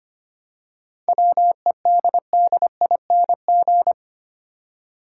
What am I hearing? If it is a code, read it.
Morse code, heard as a keyed tone, about 25 wpm, 700 Hz. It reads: WEDDING
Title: Seven Letter Words - 25wpm